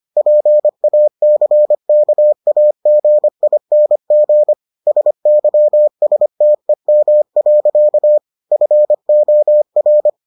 Morse code, heard as a keyed tone, about 25 wpm, 600 Hz. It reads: PACKAGING SYSTEM. FOR